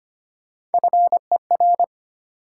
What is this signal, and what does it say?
Morse code, heard as a keyed tone, about 34 wpm, 700 Hz. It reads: FER